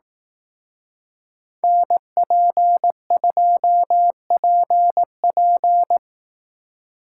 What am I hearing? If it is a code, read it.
Morse code, heard as a keyed tone, about 18 wpm, 700 Hz. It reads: NP2PP
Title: Speed Racing - USA 2x2 Call Signs 15wpm